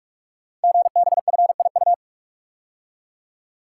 Morse code, heard as a keyed tone, about 44 wpm, 690 Hz. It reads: GBFIU
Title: Code Groups - 5 Letters - 45wpm